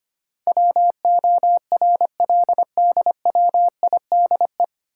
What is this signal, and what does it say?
Morse code, heard as a keyed tone, about 25 wpm, 700 Hz. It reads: WORLDWIDE
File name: Nine Letter Words - 25wpm